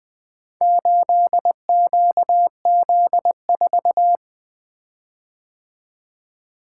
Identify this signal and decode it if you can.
Morse code, heard as a keyed tone, about 20 wpm, 700 Hz. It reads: 8QZ4
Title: Four Letter-Number Combinations - 20wpm